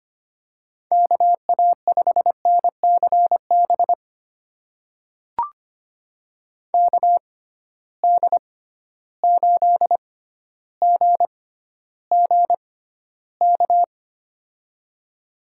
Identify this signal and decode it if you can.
Morse code, heard as a keyed tone, about 25 wpm, spaced as for 13 wpm, 700 Hz. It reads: KA5NCB KD8GGK
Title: USA Call Signs - 6 Characters - 6x Character Spacing - 25wpm